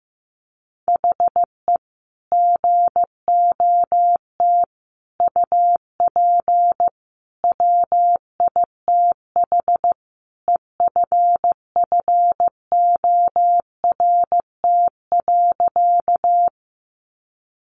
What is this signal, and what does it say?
Morse code, heard as a keyed tone, about 15 wpm, 700 Hz. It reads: HE GOT UP WITH EFFORT.
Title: Sentences from Top 1200 Words 15wpm